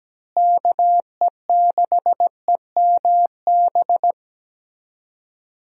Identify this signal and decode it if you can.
Morse code, heard as a keyed tone, about 17 wpm, 700 Hz. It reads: KE6EMB
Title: USA Call Signs - 6 Characters - Repeated 2x - 17wpm